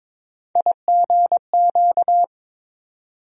Morse code, heard as a keyed tone, about 22 wpm, 700 Hz. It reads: IGQ
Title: USA Call Sign Postfixes - 3 Letters 22wpm